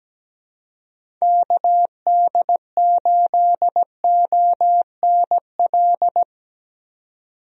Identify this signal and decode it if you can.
Morse code, heard as a keyed tone, about 17 wpm, 700 Hz. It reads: KD8ONL
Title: USA Call Signs - 6 Characters - 17wpm